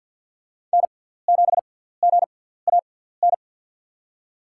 Morse code, heard as a keyed tone, about 52 wpm, spaced as for 17 wpm, 700 Hz. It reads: N7GAN
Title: USA Call Signs - 6x Character Spacing - 50wpm